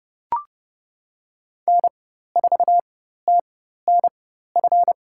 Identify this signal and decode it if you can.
Morse code, heard as a keyed tone, about 30 wpm, spaced as for 14 wpm, 700 Hz. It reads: N4TNF
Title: USA Call Signs - 4x Character Spacing - 30wpm